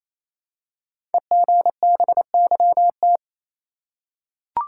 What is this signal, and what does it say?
Morse code, heard as a keyed tone, about 27 wpm, 700 Hz. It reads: EGBYT
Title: Code Groups - 5 Letter:Numbers - 28wpm